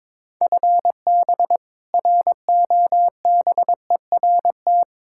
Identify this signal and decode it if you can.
Morse code, heard as a keyed tone, about 22 wpm, 700 Hz. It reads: FB ROBERT
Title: K1USN SST QSO Practice 22wpm